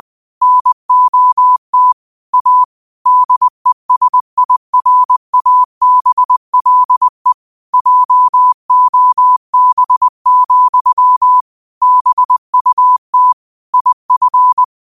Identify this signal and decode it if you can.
Morse code, heard as a keyed tone, about 20 wpm, 1000 Hz. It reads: NOT A DESIRABLE JOB, BUT IF
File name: qotdfiles / 20wpm